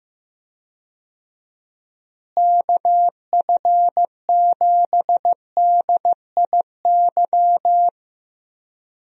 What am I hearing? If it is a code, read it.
Morse code, heard as a keyed tone, about 15 wpm, 700 Hz. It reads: KF7DIY
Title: USA Call Signs - 6 Characters - Repeated 2x - 15wpm